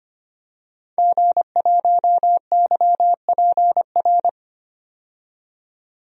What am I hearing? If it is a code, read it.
Morse code, heard as a keyed tone, about 25 wpm, 700 Hz. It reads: G1YPR
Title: DXCC-100 Call Signs - 5 Characters - 25wpm